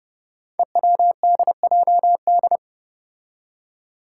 Morse code, heard as a keyed tone, about 28 wpm, 700 Hz. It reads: EWDJD
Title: Five Challenging Letter-Number Combinations 30wpm